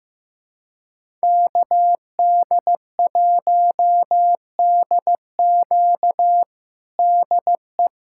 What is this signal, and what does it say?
Morse code, heard as a keyed tone, about 16 wpm, 700 Hz. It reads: KD1DQ DE
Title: USA Callsign DE Callsign - 5 Characters - 15wpm